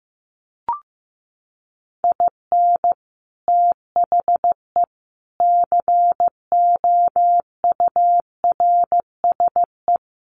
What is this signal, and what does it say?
Morse code, heard as a keyed tone, about 15 wpm, 700 Hz. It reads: IN THE COURSE
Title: Sets of 5 Words 15wpm